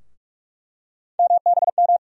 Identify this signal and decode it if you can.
Morse code, heard as a keyed tone, about 46 wpm, 700 Hz. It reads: MBM
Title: USA Call Sign Postfixes - 3 Letters 45wpm